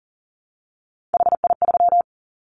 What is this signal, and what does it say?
Morse code, heard as a keyed tone, about 40 wpm, 700 Hz. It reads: HI3
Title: Callsign Prefixes for Top 100 DXCC - 40wpm